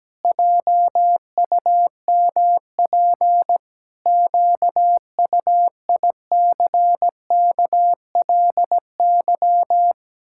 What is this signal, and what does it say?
Morse code, heard as a keyed tone, about 17 wpm, 700 Hz. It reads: JUMP QUICKLY
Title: Pangram Challenge v2 - 17wpm